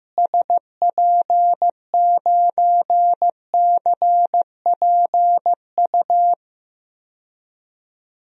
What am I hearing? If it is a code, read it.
Morse code, heard as a keyed tone, about 15 wpm, 700 Hz. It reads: SP9CPU
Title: DXCC-100 Call Signs - 6 Characters - Repeated 3x - 15wpm